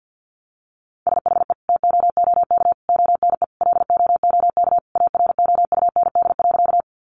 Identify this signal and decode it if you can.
Morse code, heard as a keyed tone, about 50 wpm, 700 Hz. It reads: SHE TOOK ONE LOOK AROUND.
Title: Sentences from Top 600 Words 50wpm